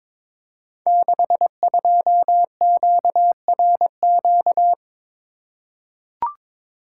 Morse code, heard as a keyed tone, about 22 wpm, 690 Hz. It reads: 62QRQ
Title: Code Groups - 5 Letter:Numbers - 22wpm